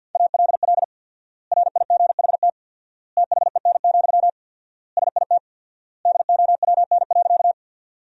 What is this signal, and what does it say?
Morse code, heard as a keyed tone, about 53 wpm, 700 Hz. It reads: ALL RIGHT THEN, SIT DOWN.
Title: 4x Word-Spacing - Sentences from Top 500 Words - 50wpm